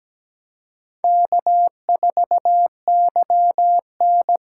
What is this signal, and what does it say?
Morse code, heard as a keyed tone, about 20 wpm, 700 Hz. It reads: K4YN
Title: Speed Racing - USA Call Signs 17wpm